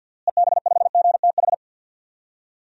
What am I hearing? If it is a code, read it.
Morse code, heard as a keyed tone, about 48 wpm, 700 Hz. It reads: EB5GTH
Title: DXCC-100 Call Signs - 6 Characters - Repeated 3x - 50wpm